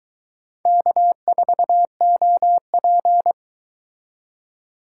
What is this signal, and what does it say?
Morse code, heard as a keyed tone, about 22 wpm, 700 Hz. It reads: K4OP